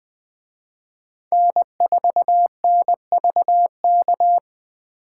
Morse code, heard as a keyed tone, about 20 wpm, 700 Hz. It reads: N4NVK